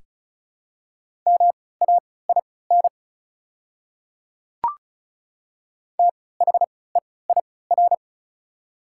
Morse code, heard as a keyed tone, about 34 wpm, spaced as for 20 wpm, 700 Hz. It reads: MAIN THEIR
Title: 3x Character Spacing - Top 300 Words - 35wpm